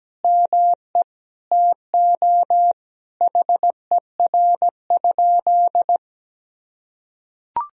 Morse code, heard as a keyed tone, about 17 wpm, 700 Hz. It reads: ME TO HER?
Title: Sentences from Top 200 Words and Abbreviations 17wpm